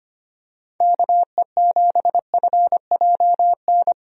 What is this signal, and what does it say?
Morse code, heard as a keyed tone, about 25 wpm, 700 Hz. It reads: KE7FJN
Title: USA Call Signs - 6 Characters - 25wpm